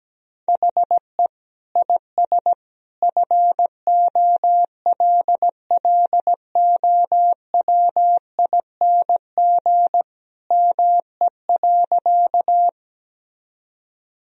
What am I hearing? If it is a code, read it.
Morse code, heard as a keyed tone, about 17 wpm, 700 Hz. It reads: HE IS FOLLOWING ME.
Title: Sentences from Top 1000 Words - Review Part 4 - 17wpm